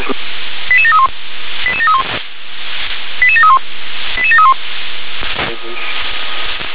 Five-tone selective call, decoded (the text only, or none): 88731 88731 88731 88731